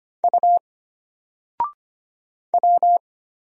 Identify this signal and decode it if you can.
Morse code, heard as a keyed tone, about 25 wpm, 700 Hz. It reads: U W
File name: Single-Letter-Number-25wpm